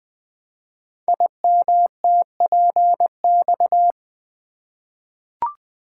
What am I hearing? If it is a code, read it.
Morse code, heard as a keyed tone, about 20 wpm, 700 Hz. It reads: IMTPX
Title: Code Groups - 5 Letters - 20wpm